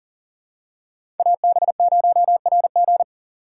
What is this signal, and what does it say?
Morse code, heard as a keyed tone, about 39 wpm, 700 Hz. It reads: AB0RG
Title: USA Call Signs - 5 Characters - 6x Character Spacing - 40wpm